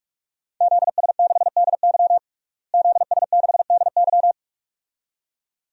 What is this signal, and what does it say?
Morse code, heard as a keyed tone, about 42 wpm, 700 Hz. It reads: ZS6DY ZS6DY
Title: DXCC-100 Call Signs - 5 Characters - Repeated 2x - 45wpm